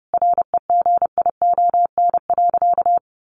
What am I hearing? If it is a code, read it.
Morse code, heard as a keyed tone, about 30 wpm, 700 Hz. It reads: REGION.
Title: A Sound Of Thunder 30wpm